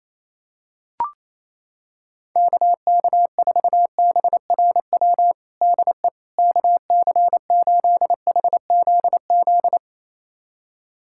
Morse code, heard as a keyed tone, about 28 wpm, 700 Hz. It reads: KK4BRW DE KC8HZZ
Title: USA Callsign DE Callsign - 6 Characters - 28wpm